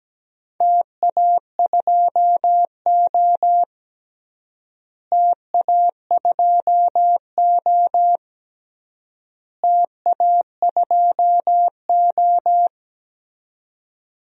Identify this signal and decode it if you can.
Morse code, heard as a keyed tone, about 17 wpm, 700 Hz. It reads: TA2O TA2O TA2O
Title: DXCC-100 Call Signs - 4 Characters - Repeated 3x - 17wpm